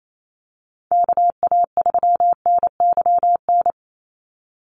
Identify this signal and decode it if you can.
Morse code, heard as a keyed tone, about 28 wpm, 700 Hz. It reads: KA3NYN
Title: USA Call Signs 28wpm